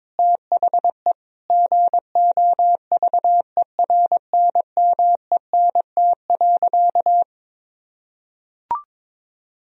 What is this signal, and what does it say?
Morse code, heard as a keyed tone, about 22 wpm, 700 Hz. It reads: THE GOVERNMENT.
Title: Sentences from Top 1000 Words - Review Part 3 - 22wpm